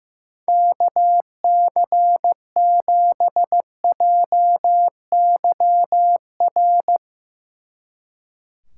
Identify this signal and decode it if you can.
Morse code, heard as a keyed tone, about 16 wpm, 700 Hz. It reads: KC7JYR
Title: Speed Racing - USA 2x3 Call Signs 15wpm